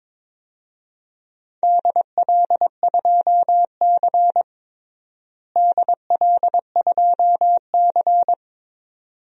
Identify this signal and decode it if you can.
Morse code, heard as a keyed tone, about 22 wpm, 700 Hz. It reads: DL2C DL2C
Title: DXCC-100 Call Signs - 4 Characters - Repeated 2x - 22wpm